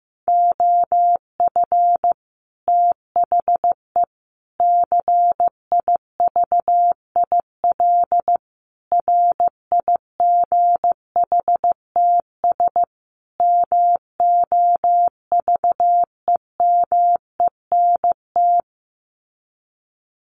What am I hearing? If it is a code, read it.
Morse code, heard as a keyed tone, about 15 wpm, 700 Hz. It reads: OF THE CIVIL RIGHTS MOVEMENT
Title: Sets of 5 Words Encore2 15wpm